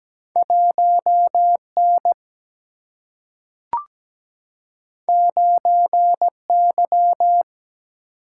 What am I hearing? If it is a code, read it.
Morse code, heard as a keyed tone, about 17 wpm, 700 Hz. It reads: 1 N 9 Y
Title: Single-Letter Single-Number 17wpm